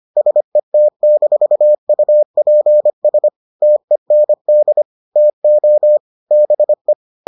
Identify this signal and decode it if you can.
Morse code, heard as a keyed tone, about 25 wpm, 600 Hz. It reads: SET-UPS TEND TO BE